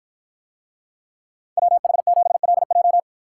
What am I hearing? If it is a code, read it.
Morse code, heard as a keyed tone, about 53 wpm, 700 Hz. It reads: WH7LJ